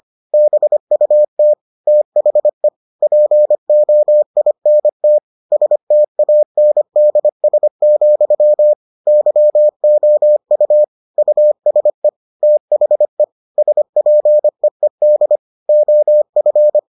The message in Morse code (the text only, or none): BUT THE POINT STANDS, YOU USE THE SPEED OF